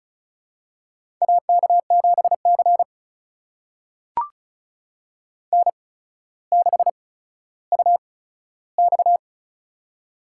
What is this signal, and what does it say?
Morse code, heard as a keyed tone, about 36 wpm, spaced as for 15 wpm, 700 Hz. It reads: AK7C N6UX
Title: USA Call Signs - 4 Characters - 8x Character Spacing - 35wpm